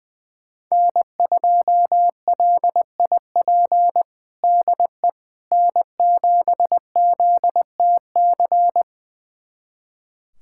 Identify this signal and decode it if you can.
Morse code, heard as a keyed tone, about 20 wpm, 700 Hz. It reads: N2LIP DE N7ZTC
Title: USA Callsign DE Callsign - 5 Characters - 20wpm